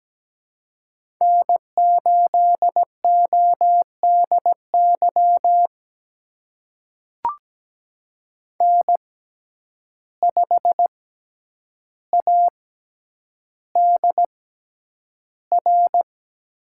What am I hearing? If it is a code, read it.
Morse code, heard as a keyed tone, about 17 wpm, spaced as for 9 wpm, 700 Hz. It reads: N8ODY N5ADR
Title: USA Call Signs - 5 Characters - 6x Character Spacing - 17wpm